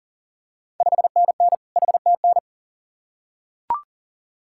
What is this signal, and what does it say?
Morse code, heard as a keyed tone, about 40 wpm, 700 Hz. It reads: 5NN HTN
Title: Ontario QSO Party Exchange 40wpm